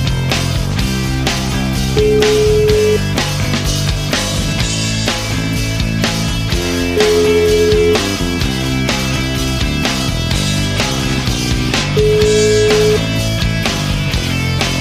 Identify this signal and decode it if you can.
Ringback tone (Europe, 425 Hz), with noise.